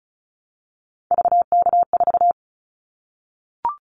Morse code, heard as a keyed tone, about 36 wpm, 700 Hz. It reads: VK4
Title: Callsign Prefixes for Top 50 DXCC - 35wpm